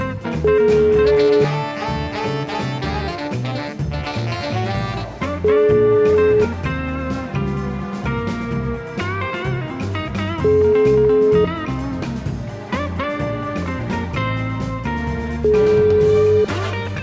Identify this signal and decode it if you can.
Ringback tone (Europe, 425 Hz), with noise.